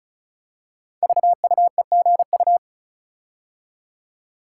Morse code, heard as a keyed tone, about 36 wpm, 700 Hz. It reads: VUEGU
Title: Code Groups - 5 Letters - 35wpm